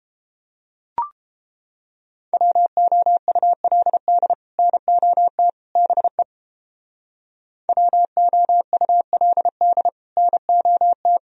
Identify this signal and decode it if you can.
Morse code, heard as a keyed tone, about 31 wpm, 700 Hz. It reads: WOULD NOT BE WOULD NOT